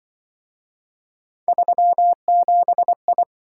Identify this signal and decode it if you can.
Morse code, heard as a keyed tone, about 25 wpm, 700 Hz. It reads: 37I
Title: Speed-Racing - Three Letter-Number Combinations - 20wpm